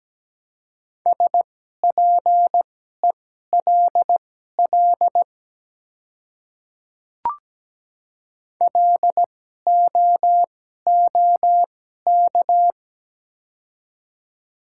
Morse code, heard as a keyed tone, about 17 wpm, spaced as for 12 wpm, 700 Hz. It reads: SPELL LOOK
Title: 2x Character Spacing - Top 300 Words - 17wpm